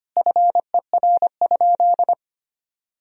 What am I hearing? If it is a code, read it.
Morse code, heard as a keyed tone, about 25 wpm, 700 Hz. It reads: FER?
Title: Sentences from Top 200 Words and Abbreviations 25wpm